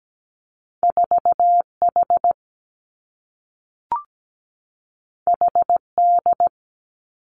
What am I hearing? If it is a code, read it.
Morse code, heard as a keyed tone, about 17 wpm, 700 Hz. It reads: 4H HD
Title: Two Challenging Letter-Number Combinations 17wpm